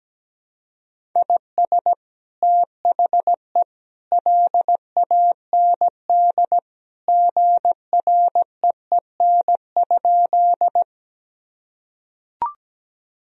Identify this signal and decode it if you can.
Morse code, heard as a keyed tone, about 17 wpm, 700 Hz. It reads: IS THE LAND GREEN?
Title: Sentences from Top 1000 Words - Review Part 2 - 17wpm